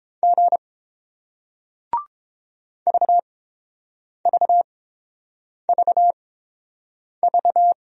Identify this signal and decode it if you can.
Morse code, heard as a keyed tone, about 30 wpm, spaced as for 7 wpm, 700 Hz. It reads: G VVVV